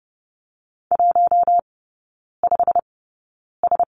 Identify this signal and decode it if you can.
Morse code, heard as a keyed tone, about 30 wpm, spaced as for 9 wpm, 700 Hz. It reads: 15S
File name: Lesson 18e - Letter m - Three Characters 30wpm